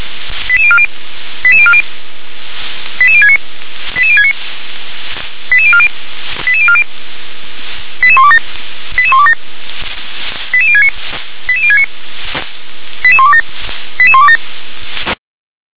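Five-tone selective call (ZVEI), heard as quick runs of five tones, tones of about 70 ms. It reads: 88040 88040 88069 88069 88040 88040 88127 88127 88069 88069 88127 88127
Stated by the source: ZVEI five-tone selective call 88186 received on an SDR.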